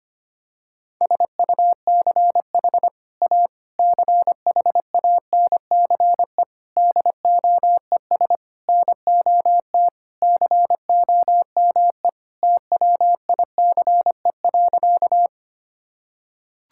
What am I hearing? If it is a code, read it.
Morse code, heard as a keyed tone, about 25 wpm, 700 Hz. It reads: SUCH A CHANCE DOES NOT COME TWICE.
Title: Sentences from Top 1600 Words 25wpm